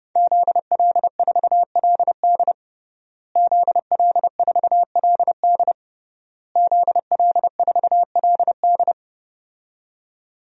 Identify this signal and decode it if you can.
Morse code, heard as a keyed tone, about 30 wpm, 700 Hz. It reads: ZL4LD ZL4LD ZL4LD
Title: DXCC-100 Call Signs - 5 Characters - Repeated 3x - 30wpm